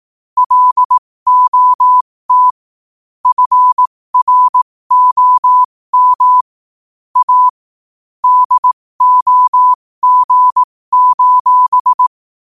Morse code, heard as a keyed tone, about 18 wpm, spaced as for 15 wpm, 1000 Hz. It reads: LOT FROM A DOG:
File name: qotdfiles / 15wpm